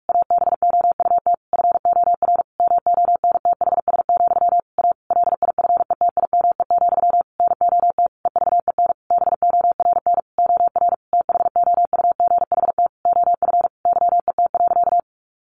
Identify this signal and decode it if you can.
Morse code, heard as a keyed tone, about 44 wpm, 700 Hz. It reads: ABOUT FOR MONTHS, A LIFETIME, NOT EVEN BORN OR THOUGHT OF YET.
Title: A Sound Of Thunder 45wpm